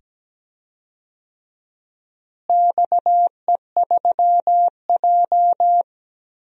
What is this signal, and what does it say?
Morse code, heard as a keyed tone, about 17 wpm, 700 Hz. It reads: XE3J
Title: DXCC-100 Call Signs - 4 Characters - Repeated 2x - 17wpm